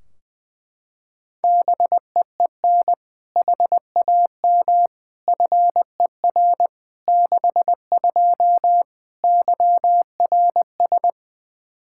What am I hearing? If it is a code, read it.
Morse code, heard as a keyed tone, about 20 wpm, 700 Hz. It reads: BEEN HAM FER 62 YRS
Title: K1USN SST QSO Practice 20wpm